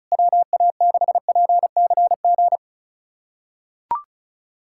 Morse code, heard as a keyed tone, about 36 wpm, 700 Hz. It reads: WA6PCG
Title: USA Callsign DE Callsign 35wpm